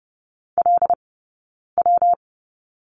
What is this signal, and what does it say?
Morse code, heard as a keyed tone, about 30 wpm, spaced as for 9 wpm, 700 Hz. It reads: LW
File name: Lesson 20d - Number 3 - Two Characters 30wpm